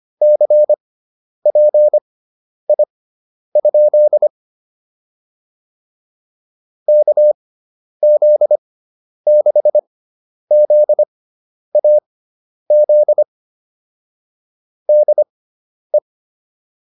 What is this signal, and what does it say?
Morse code, heard as a keyed tone, about 25 wpm, spaced as for 10 wpm, 600 Hz. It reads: CPI? KZ6ZAZ DE